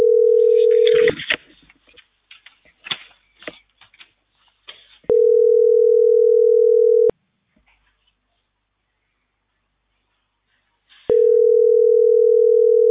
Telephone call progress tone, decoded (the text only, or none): ringback tone